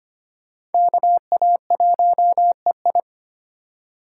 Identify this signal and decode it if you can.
Morse code, heard as a keyed tone, about 31 wpm, 700 Hz. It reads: KA1EI